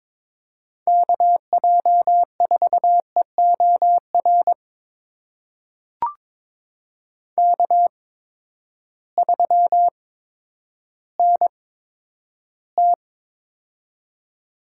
Morse code, heard as a keyed tone, about 23 wpm, spaced as for 11 wpm, 700 Hz. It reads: KJ4EOR K3NT
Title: USA Call Signs - 8x Character Spacing - 22wpm